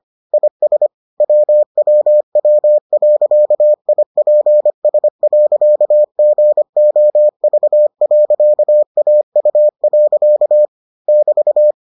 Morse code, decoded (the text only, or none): IS WWW.IPS.GOV.AU. <BT>